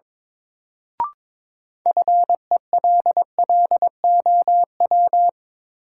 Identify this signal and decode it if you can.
Morse code, heard as a keyed tone, about 22 wpm, 700 Hz. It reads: FELLOW
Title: Sets of 5 Words Encore - New Word Drill - 22wpm